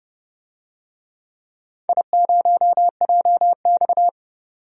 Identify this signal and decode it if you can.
Morse code, heard as a keyed tone, about 29 wpm, 700 Hz. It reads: I0JX
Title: DXCC-100 Call Signs - 4 Characters - 30wpm